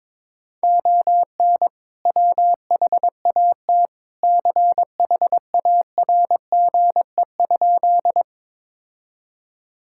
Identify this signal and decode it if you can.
Morse code, heard as a keyed tone, about 22 wpm, 700 Hz. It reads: ON WHAT CHARGE?
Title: Sentences from Top 1000 Words - Review Part 6 - 22wpm